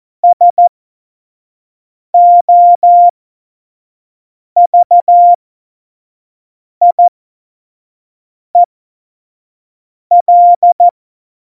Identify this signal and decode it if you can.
Morse code, heard as a keyed tone, about 14 wpm, spaced as for 5 wpm, 700 Hz. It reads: SOVIEL
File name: Morsekurs / Witze / Witz 08. August 17, Tempo 25 F70